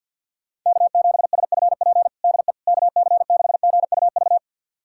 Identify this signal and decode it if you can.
Morse code, heard as a keyed tone, about 52 wpm, 700 Hz. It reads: K7SFP DE KC6GUV
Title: USA Callsign DE Callsign 50wpm